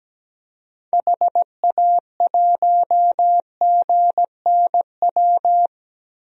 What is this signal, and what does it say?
Morse code, heard as a keyed tone, about 17 wpm, 700 Hz. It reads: HA1GNW
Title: DXCC-100 Call Signs - 6 Characters - 17wpm